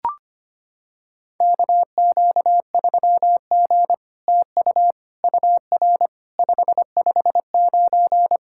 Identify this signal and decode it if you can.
Morse code, heard as a keyed tone, about 25 wpm, 700 Hz. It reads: KQ3G TU UR 559
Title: SOTA QSO Practice 25wpm